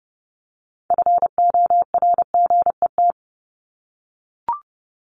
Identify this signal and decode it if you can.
Morse code, heard as a keyed tone, about 30 wpm, 700 Hz. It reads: FORGET
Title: Top 1001-1250 Words 30wpm